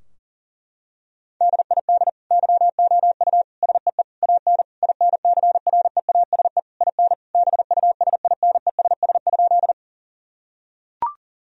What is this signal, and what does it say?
Morse code, heard as a keyed tone, about 40 wpm, 700 Hz. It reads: DID YOU SEE AN INCREASE IN BUSINESS?
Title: Sentences from Top 1000 Words - Review Part 4 - 40wpm